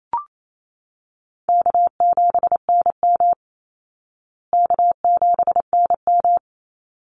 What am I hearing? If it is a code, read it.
Morse code, heard as a keyed tone, about 28 wpm, 700 Hz. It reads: K7NM K7NM
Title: Easy Call Signs Repeated 2x - 28wpm